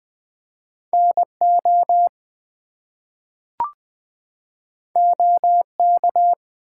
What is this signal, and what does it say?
Morse code, heard as a keyed tone, about 20 wpm, 700 Hz. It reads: NO OK
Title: Two Letter Words - 20wpm